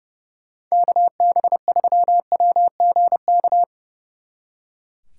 Morse code, heard as a keyed tone, about 37 wpm, 700 Hz. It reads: KB3WGK
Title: Speed Racing - USA 2x3 Call Signs 30wpm